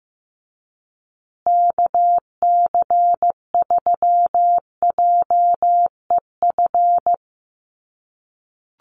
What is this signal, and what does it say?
Morse code, heard as a keyed tone, about 15 wpm, 700 Hz. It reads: KC3JEF
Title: USA Call Signs Repeated 3x 15wpm